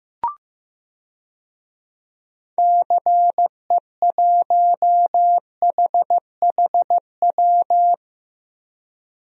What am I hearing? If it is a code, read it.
Morse code, heard as a keyed tone, about 15 wpm, 700 Hz. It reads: CE1HHW
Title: DXCC-100 Call Signs - 6 Characters - Repeated 3x - 15wpm